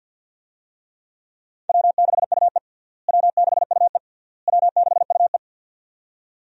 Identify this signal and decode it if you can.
Morse code, heard as a keyed tone, about 52 wpm, 700 Hz. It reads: W6UE W6UE W6UE
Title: USA Call Signs - 4 Characters - Repeated 3x - 50wpm